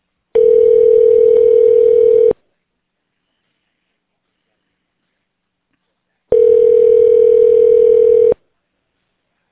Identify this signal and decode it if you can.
Ringback tone (North America), 440 + 480 Hz.